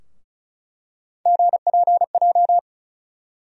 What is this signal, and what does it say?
Morse code, heard as a keyed tone, about 36 wpm, 700 Hz. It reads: GPJ
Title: USA Call Sign Postfixes - 3 Letters 35wpm